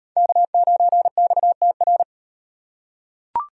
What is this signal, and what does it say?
Morse code, heard as a keyed tone, about 36 wpm, 700 Hz. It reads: K9XTR